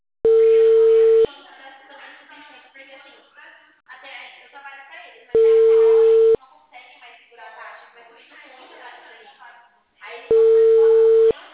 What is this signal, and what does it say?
Ringback tone (China, 450 Hz), with speech.